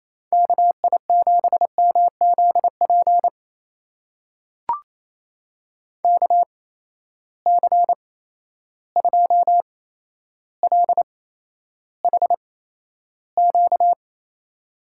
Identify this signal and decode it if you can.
Morse code, heard as a keyed tone, about 27 wpm, spaced as for 12 wpm, 700 Hz. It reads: KI7MZP KC2LHQ
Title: USA Call Signs - 6 Characters - 8x Character Spacing - 28wpm